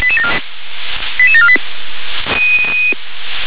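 Five-tone selective call (ZVEI), heard as quick runs of five tones, tones of about 70 ms. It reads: 88748
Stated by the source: ZVEI five-tone selective call received by SDR.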